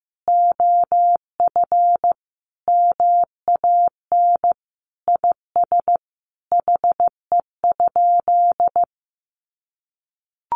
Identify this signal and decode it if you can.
Morse code, heard as a keyed tone, about 15 wpm, 700 Hz. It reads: OF MAN IS HE?
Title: Sentences from Top 1300 Words 15wpm